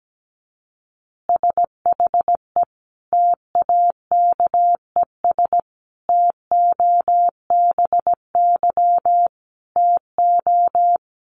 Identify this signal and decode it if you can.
Morse code, heard as a keyed tone, about 17 wpm, 700 Hz. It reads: SHE TAKES TOBY TO
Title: Easy Phrases 17wpm